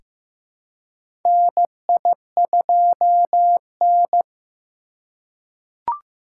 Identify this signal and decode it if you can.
Morse code, heard as a keyed tone, about 15 wpm, 700 Hz. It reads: NI2N
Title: USA Call Signs - 2x Character Spacing - 15wpm